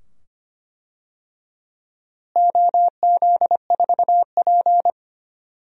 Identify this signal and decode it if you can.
Morse code, heard as a keyed tone, about 25 wpm, 700 Hz. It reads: OZ4P
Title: DXCC-100 Call Signs - 4 Characters - Repeated 2x - 25wpm